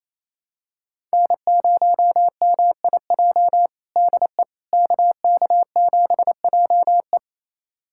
Morse code, heard as a keyed tone, about 28 wpm, 700 Hz. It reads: N0MIJ DE KK7JE
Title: USA Callsign DE Callsign - 5 Characters - 28wpm